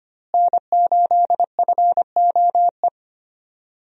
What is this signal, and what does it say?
Morse code, heard as a keyed tone, about 25 wpm, 700 Hz. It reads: N8FOE
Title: USA Call Signs - 5 Characters - Repeated 2x - 25wpm